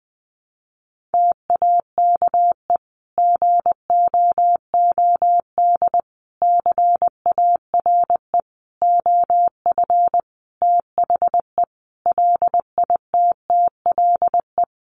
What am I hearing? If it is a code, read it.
Morse code, heard as a keyed tone, about 20 wpm, 700 Hz. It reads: TAKE GOOD CARE OF THE LITTLE
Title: Sentences from Top 400 Words 20wpm